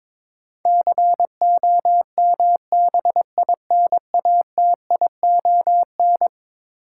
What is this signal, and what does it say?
Morse code, heard as a keyed tone, about 22 wpm, 700 Hz. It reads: COMBINATION
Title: Sentences from Top 1300 Words - New Word Drill Alt - 22wpm